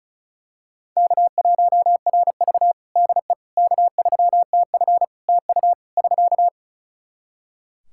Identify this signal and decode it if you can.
Morse code, heard as a keyed tone, about 36 wpm, 700 Hz. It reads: K1RV DE K3TF TU <SK>
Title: Practice QSO - Exchange 4 - Ending - 35wpm